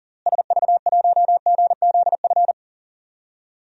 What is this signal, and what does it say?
Morse code, heard as a keyed tone, about 40 wpm, 700 Hz. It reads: SV1GZF
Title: DXCC-100 Call Signs - 6 Characters - Repeated 3x - 40wpm